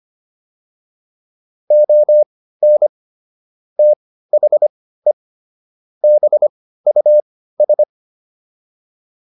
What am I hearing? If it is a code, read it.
Morse code, heard as a keyed tone, about 25 wpm, spaced as for 15 wpm, 600 Hz. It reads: ON THE BUS